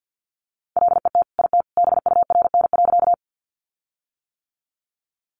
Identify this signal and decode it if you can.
Morse code, heard as a keyed tone, about 50 wpm, 700 Hz. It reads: LET IT BURN.
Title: Sentences from Top 1300 Words 50wpm